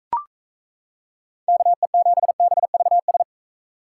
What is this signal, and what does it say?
Morse code, heard as a keyed tone, about 34 wpm, 700 Hz. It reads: KE7BVS
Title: Speed Racing - USA Call Signs 28wpm